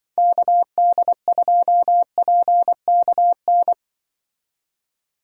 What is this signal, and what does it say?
Morse code, heard as a keyed tone, about 23 wpm, 700 Hz. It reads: KD2PKN
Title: Speed Racing - USA 2x3 Call Signs 20wpm